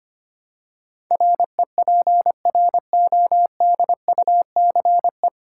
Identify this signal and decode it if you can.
Morse code, heard as a keyed tone, about 25 wpm, 700 Hz. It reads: REPRODUCE
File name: Nine Letter Words - 25wpm